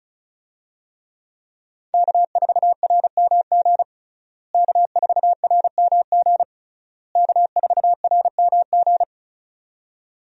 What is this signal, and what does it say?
Morse code, heard as a keyed tone, about 36 wpm, 700 Hz. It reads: K4RMG K4RMG K4RMG
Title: USA Call Signs - 5 Characters - Repeated 3x - 35wpm